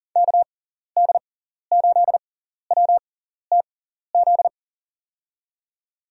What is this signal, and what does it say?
Morse code, heard as a keyed tone, about 40 wpm, spaced as for 14 wpm, 700 Hz. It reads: KD8WTZ
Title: USA Call Signs - 6 Characters - 6x Character Spacing - 40wpm